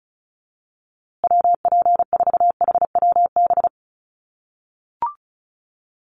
Morse code, heard as a keyed tone, about 34 wpm, 700 Hz. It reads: WP4HWB
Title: USA Call Signs Repeated 2x 35wpm